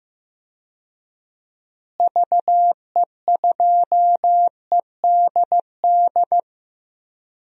Morse code, heard as a keyed tone, about 16 wpm, 700 Hz. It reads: VE2EDD
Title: DXCC-100 Call Signs - 6 Characters - Repeated 2x - 15wpm